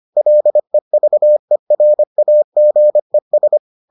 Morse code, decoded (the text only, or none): LEVERAGES